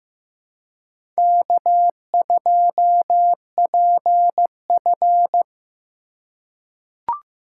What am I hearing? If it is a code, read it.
Morse code, heard as a keyed tone, about 15 wpm, 700 Hz. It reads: K2PF
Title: USA Call Signs - 2x Character Spacing - 15wpm